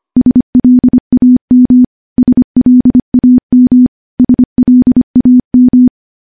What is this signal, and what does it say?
Morse code, heard as a keyed tone, about 25 wpm, 250 Hz. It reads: SLAM SLAM SLAM
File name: morse / morse-training-25WPM